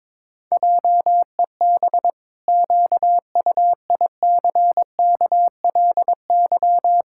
Morse code, heard as a keyed tone, about 21 wpm, 700 Hz. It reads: JEB QUICKLY
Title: Pangram Challenge v2 - 22wpm